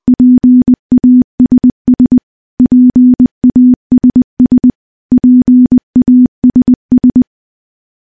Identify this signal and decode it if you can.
Morse code, heard as a keyed tone, about 20 wpm, 250 Hz. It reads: PASS PASS PASS
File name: morse / morse-training-20WPM